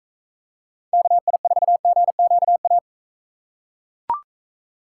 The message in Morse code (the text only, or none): KI4GQA